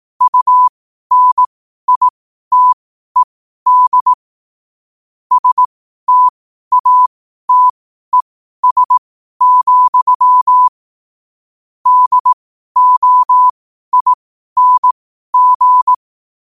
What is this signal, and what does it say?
Morse code, heard as a keyed tone, about 18 wpm, spaced as for 12 wpm, 1000 Hz. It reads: UNITED STATES, DOING
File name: qotdfiles / 12wpm